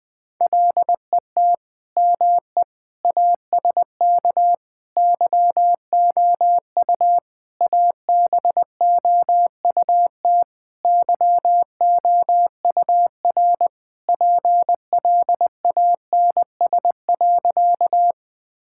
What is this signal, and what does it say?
Morse code, heard as a keyed tone, about 20 wpm, 700 Hz. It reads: LET ME ASK YOU ABOUT YOUR PLANS.
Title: Sentences with 5-Word Phrases - 20wpm